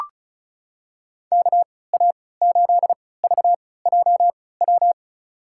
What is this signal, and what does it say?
Morse code, heard as a keyed tone, about 36 wpm, spaced as for 20 wpm, 700 Hz. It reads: KA8VJW
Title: USA Call Signs - 3x Character Spacing - 35wpm